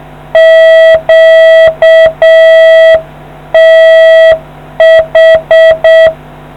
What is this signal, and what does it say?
Morse code, heard as a keyed tone, about 6 wpm, 650 Hz. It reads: QTH